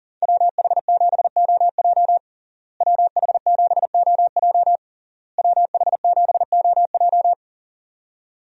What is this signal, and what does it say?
Morse code, heard as a keyed tone, about 40 wpm, 700 Hz. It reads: WH7OJ WH7OJ WH7OJ
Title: USA Call Signs - 5 Characters - Repeated 3x - 40wpm